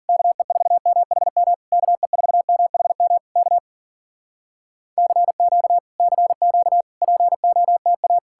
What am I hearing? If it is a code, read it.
Morse code, heard as a keyed tone, about 54 wpm, 700 Hz. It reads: KE4MHM KE4MHM K CQ CQ POTA